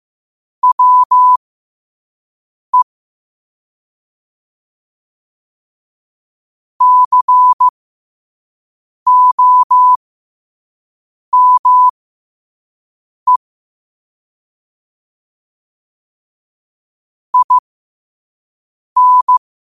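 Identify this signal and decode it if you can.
Morse code, heard as a keyed tone, about 15 wpm, spaced as for 5 wpm, 1000 Hz. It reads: WE COME IN
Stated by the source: Morse code sent at 5 wpm.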